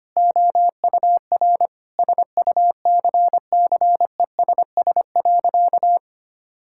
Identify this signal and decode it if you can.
Morse code, heard as a keyed tone, about 25 wpm, 700 Hz. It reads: OUR SUCCESS.
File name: practice / 25wpm / Sentences with 9-Word Phrases - 25wpm